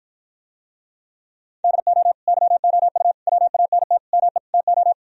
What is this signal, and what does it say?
Morse code, heard as a keyed tone, about 56 wpm, 700 Hz. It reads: DO YOU WANT ME TO